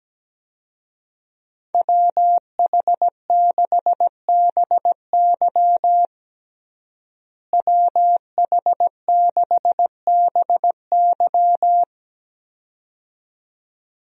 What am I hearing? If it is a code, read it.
Morse code, heard as a keyed tone, about 17 wpm, 700 Hz. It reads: WH6BY WH6BY
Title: USA Call Signs - 5 Characters - Repeated 2x - 17wpm